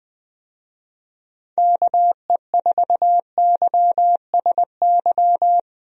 Morse code, heard as a keyed tone, about 21 wpm, 700 Hz. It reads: KE4YSY
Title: USA Call Signs - 6 Characters - Repeated 2x - 20wpm